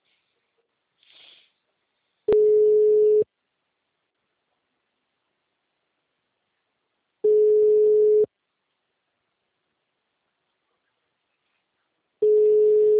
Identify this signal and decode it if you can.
Ringback tone (Europe, 425 Hz).